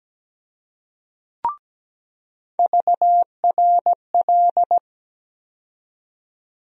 Morse code, heard as a keyed tone, about 17 wpm, 700 Hz. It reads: VRL
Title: USA Call Sign Postfixes - 3 Letters 17wpm